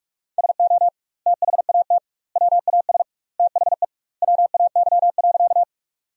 Morse code, heard as a keyed tone, about 46 wpm, 700 Hz. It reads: SO THAT WAS THE WAY.
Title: 2x Word-Spacing - Sentences from Top 100 Words - 45wpm